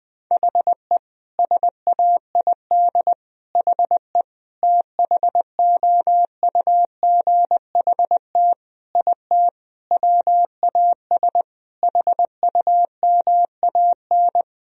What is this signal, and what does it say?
Morse code, heard as a keyed tone, about 21 wpm, 700 Hz. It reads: HE SAID HE THOUGHT IT WAS HUMAN
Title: Sentences from Top 1000 Words - Review Part 3 - 20wpm